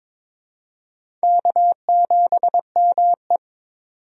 Morse code, heard as a keyed tone, about 21 wpm, 700 Hz. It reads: K7ME
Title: USA Call Signs - 4 Characters - 22wpm